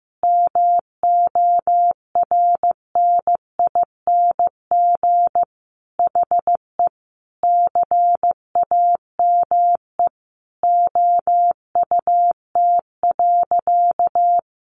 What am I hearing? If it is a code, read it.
Morse code, heard as a keyed tone, about 15 wpm, 700 Hz. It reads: MORNING HE CAME OUT.
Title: Sentences from Top 1300 Words 15wpm